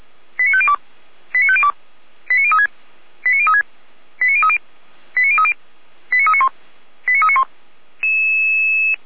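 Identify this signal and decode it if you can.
Five-tone selective call (ZVEI), heard as quick runs of five tones, tones of about 70 ms. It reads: 89502 89502 89036 89036 89030 89030 89391 89391
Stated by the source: ZVEI five-tone selective call 89187 received on SDR.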